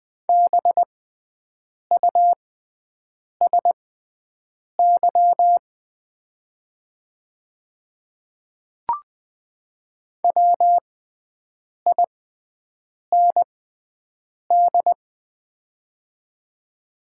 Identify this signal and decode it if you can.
Morse code, heard as a keyed tone, about 20 wpm, spaced as for 7 wpm, 700 Hz. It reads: BUSY WIND
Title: 6x Character Spacing - Top 500 Words - 20wpm